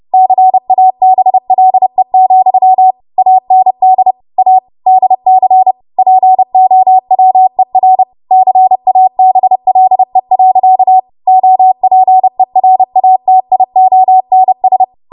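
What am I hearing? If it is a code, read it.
Morse code, heard as a keyed tone, about 30 wpm, 750 Hz. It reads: CABLE, AND A DC POWER CABLE. OPERATIONS
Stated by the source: Morse practice transmission at 30 WPM.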